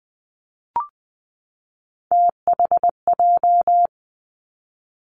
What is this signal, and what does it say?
Morse code, heard as a keyed tone, about 20 wpm, 700 Hz. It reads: THJ
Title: Three Challenging Letter-Number Combinations 20wpm